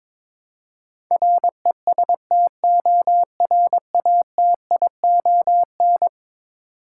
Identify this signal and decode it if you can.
Morse code, heard as a keyed tone, about 22 wpm, 700 Hz. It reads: RESTORATION
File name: practice / 22wpm / Eleven Letter Words - 22wpm